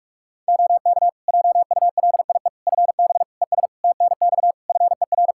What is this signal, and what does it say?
Morse code, heard as a keyed tone, about 41 wpm, 700 Hz. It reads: OK JULIE FB ES TNX FER
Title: Practice QSO - Exchange 2 - Info - 45wpm